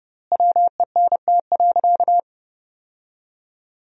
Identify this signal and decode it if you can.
Morse code, heard as a keyed tone, about 30 wpm, 700 Hz. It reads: WENT.
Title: 3x Word-Spacing - Sentences from Top 200 Words - 30wpm